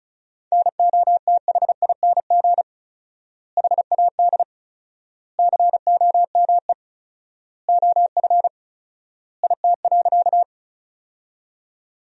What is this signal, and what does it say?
Morse code, heard as a keyed tone, about 34 wpm, 700 Hz. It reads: NOTHING HAD COME OF IT.
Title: 4x Word-Spacing - Sentences from Top 500 Words - 35wpm